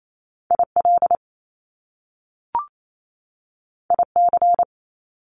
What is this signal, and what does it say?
Morse code, heard as a keyed tone, about 28 wpm, 700 Hz. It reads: IL IC
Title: Top Two Letter Combinations in Words 28wpm